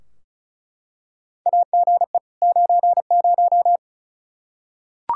Morse code, heard as a keyed tone, about 34 wpm, 700 Hz. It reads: AGE 90
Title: Practice QSO - Exchange 3 - More Info - 35wpm